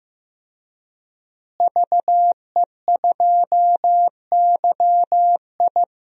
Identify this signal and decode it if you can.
Morse code, heard as a keyed tone, about 15 wpm, 700 Hz. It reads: VE2YI
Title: DXCC-100 Call Signs - 5 Characters - Repeated 3x - 15wpm